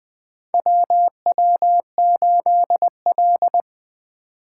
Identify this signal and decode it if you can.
Morse code, heard as a keyed tone, about 20 wpm, 700 Hz. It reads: WW8L
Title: 2023 POTA Activator Call Signs - 20wpm